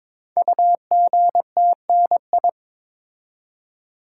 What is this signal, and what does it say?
Morse code, heard as a keyed tone, about 22 wpm, 700 Hz. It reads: UGTNI
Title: Code Groups - 5 Letters - 22wpm